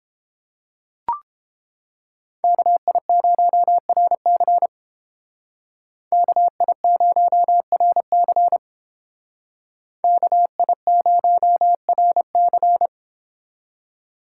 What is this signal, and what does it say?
Morse code, heard as a keyed tone, about 31 wpm, 700 Hz. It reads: KI0RC KI0RC KI0RC